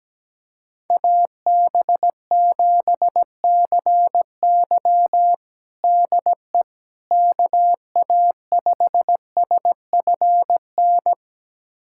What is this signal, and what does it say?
Morse code, heard as a keyed tone, about 17 wpm, 700 Hz. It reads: AB7CY DE KA5SFN
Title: USA Callsign DE Callsign 17wpm